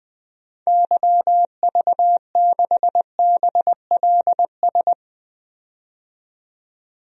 Morse code, heard as a keyed tone, about 21 wpm, 700 Hz. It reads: YV6BLS
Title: DXCC-100 Call Signs - 6 Characters - 20wpm